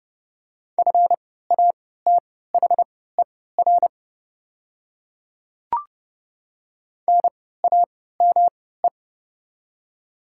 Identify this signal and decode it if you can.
Morse code, heard as a keyed tone, about 30 wpm, spaced as for 17 wpm, 690 Hz. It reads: FATHER NAME
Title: 3x Character Spacing - Top 200 Words - 30wpm